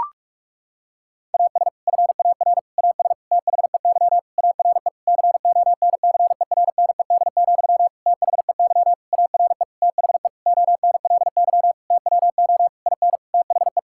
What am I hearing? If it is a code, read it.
Morse code, heard as a keyed tone, about 46 wpm, 700 Hz. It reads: AS FAR AS THEY ARE CONCERNED, THEY ARE THE ONLY TWO IN THE
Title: Sentences from Top 1000 Words - Review Part 6 - 45wpm